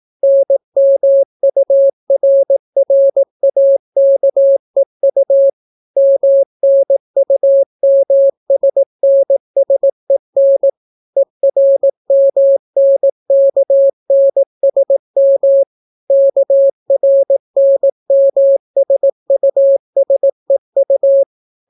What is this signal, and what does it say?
Morse code, heard as a keyed tone, about 18 wpm, 550 Hz. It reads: NMURRAKEU MNUMSNSEN ERMNKNSM KRNMSUSEU